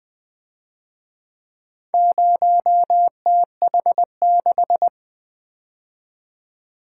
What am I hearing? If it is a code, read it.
Morse code, heard as a keyed tone, about 24 wpm, 700 Hz. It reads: 0TH6
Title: Speed-Racing - Four Letter-Number Combinations - 20wpm